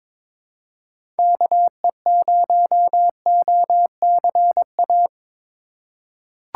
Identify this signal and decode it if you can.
Morse code, heard as a keyed tone, about 22 wpm, 700 Hz. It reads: KE0OCA
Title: USA Call Signs - 6 Characters - 3x Character Spacing - 22wpm